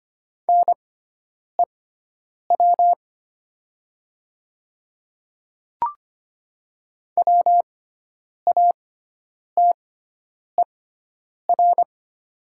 Morse code, heard as a keyed tone, about 25 wpm, spaced as for 9 wpm, 700 Hz. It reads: NEW WATER
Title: 6x Character Spacing - Top 200 Words - 25wpm